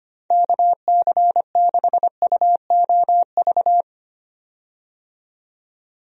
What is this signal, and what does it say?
Morse code, heard as a keyed tone, about 25 wpm, 700 Hz. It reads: KC6UOV
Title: USA Call Signs - 6 Characters - 25wpm